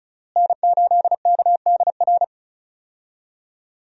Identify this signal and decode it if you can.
Morse code, heard as a keyed tone, about 36 wpm, 700 Hz. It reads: N8KDR
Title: USA Call Signs - 5 Characters - Repeated 3x - 35wpm